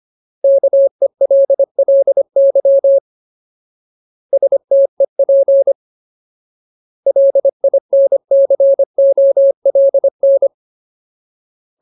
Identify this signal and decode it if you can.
Morse code, heard as a keyed tone, about 25 wpm, 550 Hz. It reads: KELLY STEP LINCOLN
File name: Ultimate Word List / 25WPM Ultimate